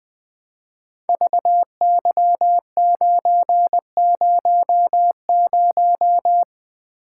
Morse code, heard as a keyed tone, about 20 wpm, 700 Hz. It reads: VY900
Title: Code Groups - 5 Letter:Numbers - 20wpm